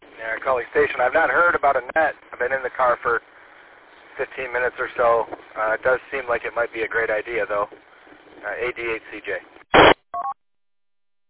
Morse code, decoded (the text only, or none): none